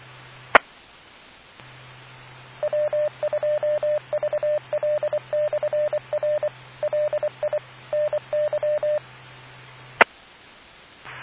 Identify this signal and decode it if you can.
Morse code, heard as a keyed tone, about 24 wpm, 600 Hz. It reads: E W2VL/R LI NY E